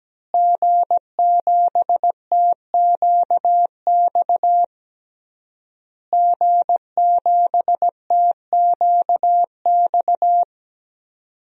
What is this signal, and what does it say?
Morse code, heard as a keyed tone, about 17 wpm, 700 Hz. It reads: G7TQX G7TQX
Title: DXCC-100 Call Signs - 5 Characters - Repeated 3x - 17wpm